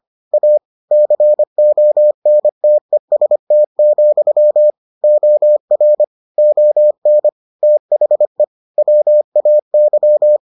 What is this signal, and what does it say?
Morse code, heard as a keyed tone, about 25 wpm, 600 Hz. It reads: A CONTEST, OR ON THE WAY